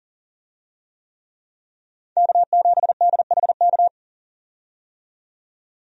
Morse code, heard as a keyed tone, about 40 wpm, 700 Hz. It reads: K7DHK
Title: USA Call Signs - 5 Characters - 40wpm